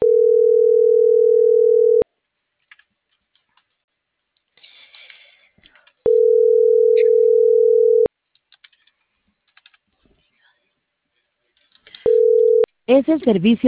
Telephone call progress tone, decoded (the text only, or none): ringback tone